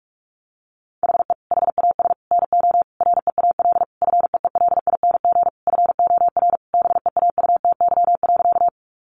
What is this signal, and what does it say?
Morse code, heard as a keyed tone, about 44 wpm, 700 Hz. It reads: HE HAS NO REAL FEELING FOR BEAUTY.
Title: Sentences from Top 1200 Words 45wpm